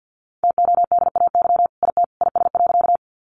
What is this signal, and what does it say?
Morse code, heard as a keyed tone, about 50 wpm, 700 Hz. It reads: TODAY IT IS.
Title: Sentences from Top 600 Words 50wpm